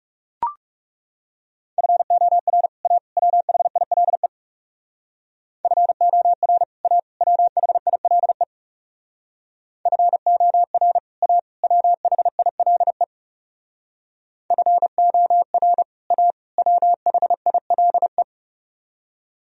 Morse code, heard as a keyed tone, about 37 wpm, 700 Hz. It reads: FOR A WHILE FOR A WHILE FOR A WHILE FOR A WHILE
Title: Speed-Racing - Sets of 3 Words 30wpm